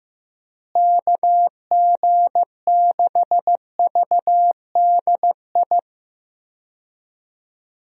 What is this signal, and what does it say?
Morse code, heard as a keyed tone, about 15 wpm, 700 Hz. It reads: KG6VDI
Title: USA Call Signs - 6 Characters - Repeated 3x - 15wpm